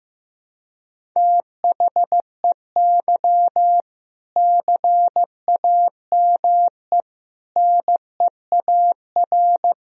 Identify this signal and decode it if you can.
Morse code, heard as a keyed tone, about 15 wpm, 700 Hz. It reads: THEY CAME NEAR
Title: Sentences from Top 1000 Words - Review Part 1 - 15wpm